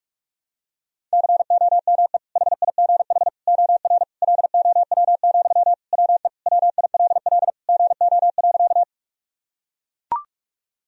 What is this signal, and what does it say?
Morse code, heard as a keyed tone, about 41 wpm, 700 Hz. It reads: COME HIGH OR LOW, WE WILL GO.
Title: Sentences from Top 1000 Words - Review Part 1 - 45wpm